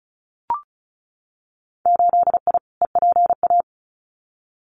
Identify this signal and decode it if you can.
Morse code, heard as a keyed tone, about 34 wpm, 700 Hz. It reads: 8I EPA
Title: Winter Field Day Exchanges Encore 35wpm